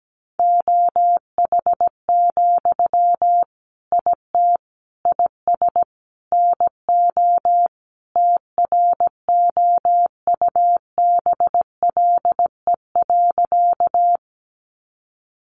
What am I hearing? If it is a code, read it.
Morse code, heard as a keyed tone, about 17 wpm, 700 Hz. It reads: OH, IT IS NO TROUBLE.
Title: Sentences from Top 1200 Words 17wpm